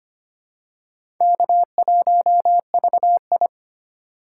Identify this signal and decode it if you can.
Morse code, heard as a keyed tone, about 25 wpm, 700 Hz. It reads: K1VI
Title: USA Call Signs - 4 Characters - 2x Character Spacing - 25wpm